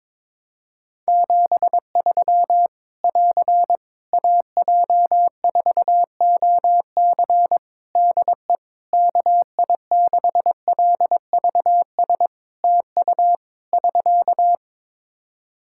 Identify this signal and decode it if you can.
Morse code, heard as a keyed tone, about 22 wpm, 700 Hz. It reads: 73 <AR> AJ4OC DE KI6LVS TU <SK>
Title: Practice QSO - Exchange 4 - Ending - 22wpm